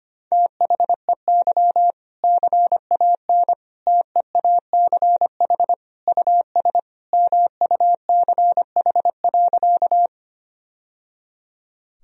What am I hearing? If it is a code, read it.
Morse code, heard as a keyed tone, about 25 wpm, 700 Hz. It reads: THEY CAN TEACH US MUCH.
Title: Sentences from Top 1000 Words - Review Part 2 - 25wpm